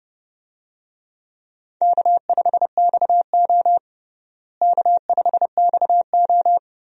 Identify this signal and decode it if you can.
Morse code, heard as a keyed tone, about 30 wpm, 700 Hz. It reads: K5XO K5XO
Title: USA Call Signs - 4 Characters - Repeated 3x - 30wpm